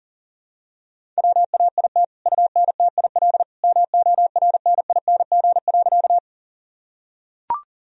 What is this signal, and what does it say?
Morse code, heard as a keyed tone, about 40 wpm, 700 Hz. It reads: WAIT UNTIL MORNING.
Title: Sentences from Top 1000 Words - Review Part 1 - 40wpm